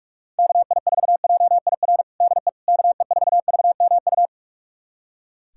Morse code, heard as a keyed tone, about 41 wpm, 700 Hz. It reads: KI4JIR DE KE4VMU
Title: USA Callsign DE Callsign - 6 Characters - 45wpm